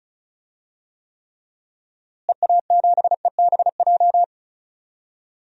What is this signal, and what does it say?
Morse code, heard as a keyed tone, about 34 wpm, 700 Hz. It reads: EA7EBJ
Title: DXCC-100 Call Signs - 6 Characters - Repeated 2x - 35wpm